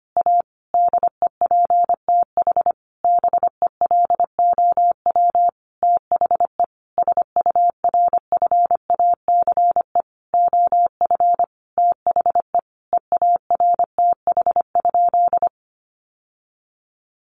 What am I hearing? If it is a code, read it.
Morse code, heard as a keyed tone, about 25 wpm, 700 Hz. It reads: A DEPTH BELOW THE SURFACE OF THE EARTH?
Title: Sentences from Top 1200 Words 25wpm